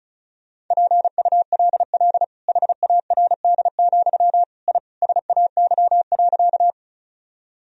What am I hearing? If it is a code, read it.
Morse code, heard as a keyed tone, about 36 wpm, 700 Hz. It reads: PULL HARD, I SAY.
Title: Sentences from Top 1000 Words - Review Part 2 - 35wpm